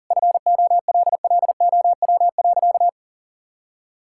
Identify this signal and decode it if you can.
Morse code, heard as a keyed tone, about 37 wpm, 700 Hz. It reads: FOLLOW.
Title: Sentences from Top 1000 Words - Review Part 6 - 40wpm